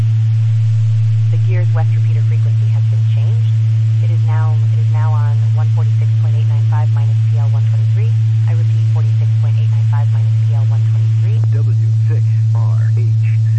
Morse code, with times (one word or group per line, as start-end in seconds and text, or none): none